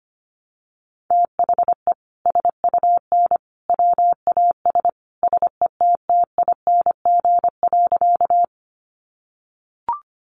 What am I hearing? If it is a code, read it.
Morse code, heard as a keyed tone, about 25 wpm, 700 Hz. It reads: THE SUN WAS SETTING.
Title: Sentences from Top 1300 Words 25wpm